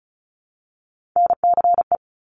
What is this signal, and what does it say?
Morse code, heard as a keyed tone, about 36 wpm, 700 Hz. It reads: NCE
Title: Top Three Letter Combinations in Words 35wpm